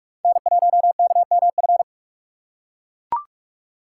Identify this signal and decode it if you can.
Morse code, heard as a keyed tone, about 41 wpm, 690 Hz. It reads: N1KMF
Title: USA Call Signs - 5 Characters - 6x Character Spacing - 45wpm